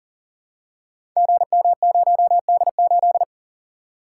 Morse code, heard as a keyed tone, about 40 wpm, 700 Hz. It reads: GM0D8
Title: Code Groups - 5 Letter:Numbers - 40wpm